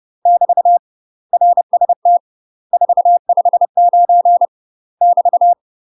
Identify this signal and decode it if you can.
Morse code, heard as a keyed tone, about 30 wpm, 700 Hz. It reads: <BT> RST 459 <BT>
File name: UK MASTER QSOs TRAINING / 30WPM / UK MASTER QSOs 0024